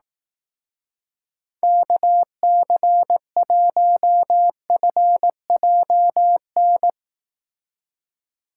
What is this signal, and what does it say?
Morse code, heard as a keyed tone, about 18 wpm, 700 Hz. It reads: KC1FJN
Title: Speed Racing - USA 2x3 Call Signs 15wpm